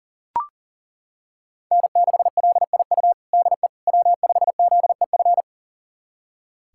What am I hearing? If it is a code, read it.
Morse code, heard as a keyed tone, about 40 wpm, 700 Hz. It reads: N6LIU DE W5ZEF
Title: USA Callsign DE Callsign - 5 Characters - 40wpm